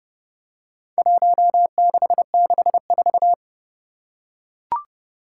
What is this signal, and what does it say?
Morse code, heard as a keyed tone, about 30 wpm, 700 Hz. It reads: 1664
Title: Four Challenging Letter-Number Combinations 30wpm